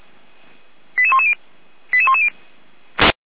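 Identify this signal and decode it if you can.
Five-tone selective call (ZVEI), heard as quick runs of five tones, tones of about 70 ms. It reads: 88110 88110
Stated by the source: ZVEI five-tone call 88109 received via SDR.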